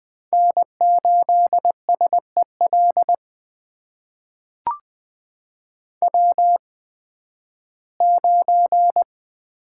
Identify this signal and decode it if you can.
Morse code, heard as a keyed tone, about 20 wpm, spaced as for 13 wpm, 700 Hz. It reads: N8SEL W9
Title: USA Call Signs - 5 Characters - 8x Character Spacing - 20wpm